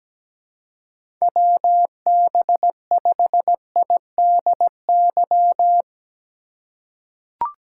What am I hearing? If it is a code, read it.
Morse code, heard as a keyed tone, about 17 wpm, 700 Hz. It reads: WB5IDY
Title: K1USN SST QSO Practice - 17wpm